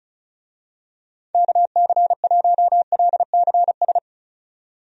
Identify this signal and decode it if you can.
Morse code, heard as a keyed tone, about 36 wpm, 700 Hz. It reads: KC1LCS
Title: USA Call Signs - 6 Characters - 35wpm